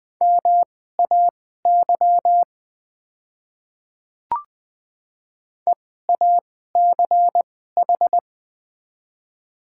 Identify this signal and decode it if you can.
Morse code, heard as a keyed tone, about 20 wpm, spaced as for 14 wpm, 690 Hz. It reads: MAY EACH
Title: 2x Character Spacing - Top 100 Words - 20wpm